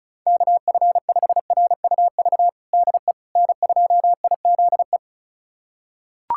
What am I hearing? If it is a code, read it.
Morse code, heard as a keyed tone, about 36 wpm, 700 Hz. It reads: KF5RUV DE N2IZE
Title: USA Callsign DE Callsign 35wpm